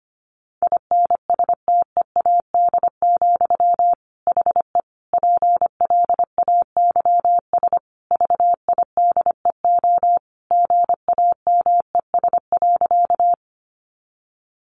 Morse code, heard as a keyed tone, about 25 wpm, 700 Hz. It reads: INSTEAD, HE PLAYS VIDEO GAMES.
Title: Additional Easy Phrases 25wpm